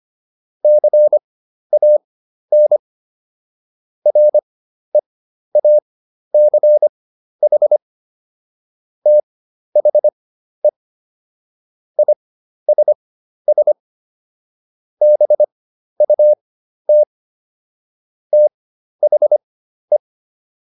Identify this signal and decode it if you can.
Morse code, heard as a keyed tone, about 25 wpm, spaced as for 12 wpm, 600 Hz. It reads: CAN REACH THE ISS BUT THE